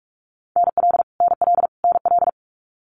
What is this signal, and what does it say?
Morse code, heard as a keyed tone, about 44 wpm, 700 Hz. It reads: NL NL NL
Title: Canadian Provinces and Territories - Repeated 3x - 45wpm